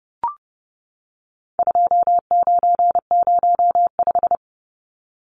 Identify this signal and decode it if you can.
Morse code, heard as a keyed tone, about 30 wpm, 700 Hz. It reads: 2905
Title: Numbers - 4-Digit 30wpm